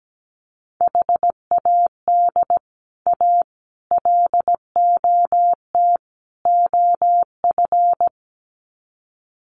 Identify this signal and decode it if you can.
Morse code, heard as a keyed tone, about 17 wpm, 700 Hz. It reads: HAD A LOT OF
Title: Sets of 4 Words 17wpm